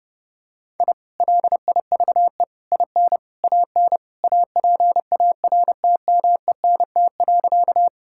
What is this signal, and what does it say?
Morse code, heard as a keyed tone, about 32 wpm, 700 Hz. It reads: I LIVE IN AN APARTMENT.
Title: Sentences from Top 1000 Words - Review Part 6 - 30wpm